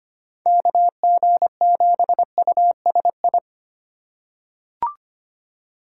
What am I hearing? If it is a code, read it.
Morse code, heard as a keyed tone, about 25 wpm, 700 Hz. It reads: KG7USI
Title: USA Call Signs - 6 Characters - 3x Character Spacing - 25wpm